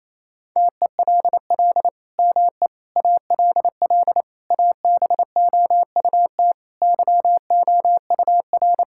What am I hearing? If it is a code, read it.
Morse code, heard as a keyed tone, about 27 wpm, 700 Hz. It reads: TELL ME ALL ABOUT YOUR
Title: Sentences from Top 1000 Words - Review Part 6 - 28wpm